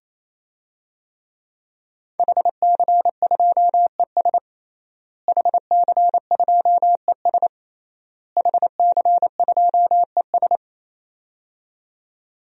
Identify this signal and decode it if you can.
Morse code, heard as a keyed tone, about 27 wpm, 700 Hz. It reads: HC2ES HC2ES HC2ES
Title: DXCC-100 Call Signs - 5 Characters - Repeated 3x - 28wpm